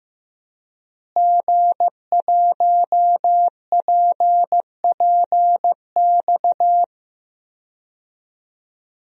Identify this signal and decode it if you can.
Morse code, heard as a keyed tone, about 15 wpm, 700 Hz. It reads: G1PPX
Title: DXCC-100 Call Signs - 5 Characters - Repeated 2x - 15wpm